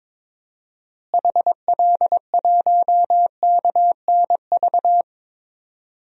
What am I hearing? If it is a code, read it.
Morse code, heard as a keyed tone, about 22 wpm, 700 Hz. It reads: HL1KNV
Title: DXCC-100 Call Signs - 6 Characters - Repeated 3x - 22wpm